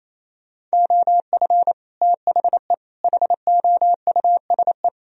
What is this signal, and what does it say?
Morse code, heard as a keyed tone, about 35 wpm, 700 Hz. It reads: OF THE HOUSE
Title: Speed-Racing - Sets of 3 Words 28wpm